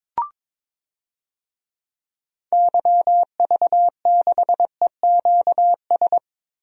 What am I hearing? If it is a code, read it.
Morse code, heard as a keyed tone, about 22 wpm, 700 Hz. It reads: YV6EQS
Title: DXCC-100 Call Signs - 6 Characters - Repeated 3x - 22wpm